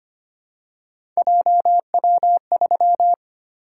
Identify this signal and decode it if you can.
Morse code, heard as a keyed tone, about 25 wpm, 700 Hz. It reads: JW3
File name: Three Letter-Number Combinations - 25wpm